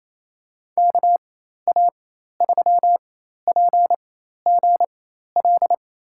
Morse code, heard as a keyed tone, about 28 wpm, spaced as for 13 wpm, 700 Hz. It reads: KA3PGL
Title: USA Call Signs - 6 Characters - 4x Character Spacing - 28wpm